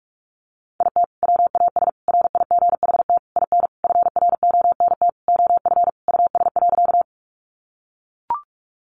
Morse code, heard as a keyed tone, about 44 wpm, 700 Hz. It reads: IT WAS RIGHT IN FRONT OF US.
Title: Sentences from Top 500 Words 45wpm